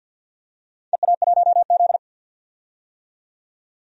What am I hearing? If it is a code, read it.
Morse code, heard as a keyed tone, about 50 wpm, 700 Hz. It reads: EA1Z
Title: DXCC-100 Call Signs - 4 Characters - 50wpm